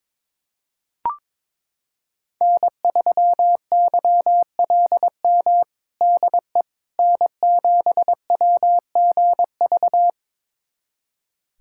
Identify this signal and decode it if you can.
Morse code, heard as a keyed tone, about 22 wpm, 700 Hz. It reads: N3YLM DE N7WGV
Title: USA Callsign DE Callsign - 5 Characters - 22wpm